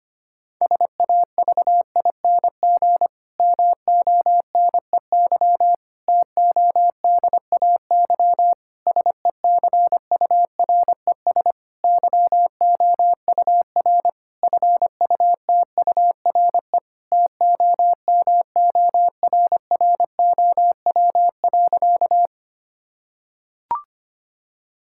Morse code, heard as a keyed tone, about 25 wpm, 700 Hz. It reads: SAVING MONEY TODAY SECURES YOUR FUTURE TOMORROW.
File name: practice / 25wpm / Difficult Sentences with Hints - 25wpm